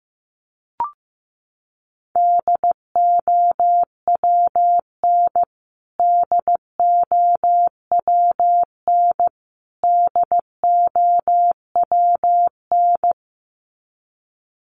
Morse code, heard as a keyed tone, about 15 wpm, 700 Hz. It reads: DOWN DOWN DOWN
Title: Top 500 Words Repeated 15wpm